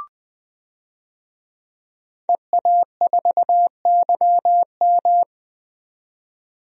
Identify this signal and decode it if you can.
Morse code, heard as a keyed tone, about 20 wpm, 700 Hz. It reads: EA4YM
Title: DXCC-100 Call Signs - 5 Characters - 20wpm